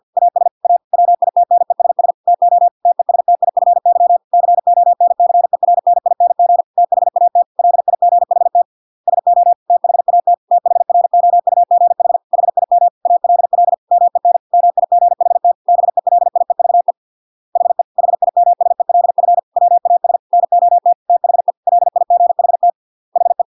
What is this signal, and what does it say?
Morse code, heard as a keyed tone, about 50 wpm, 700 Hz. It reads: AS A WITNESS TO TESTIFY CONCERNING THAT LIGHT SO THAT THROUGH HIM ALL MEN MIGHT BELIEVE HE HIMSELF WAS NOT THE LIGHT HE